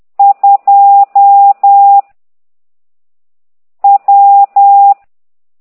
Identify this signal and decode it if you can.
Morse code, heard as a keyed tone, about 10 wpm, 800 Hz. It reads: 2 W